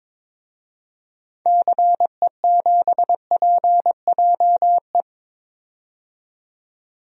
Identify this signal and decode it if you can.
Morse code, heard as a keyed tone, about 22 wpm, 700 Hz. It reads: CE7PJE
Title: DXCC-100 Call Signs - 6 Characters - 22wpm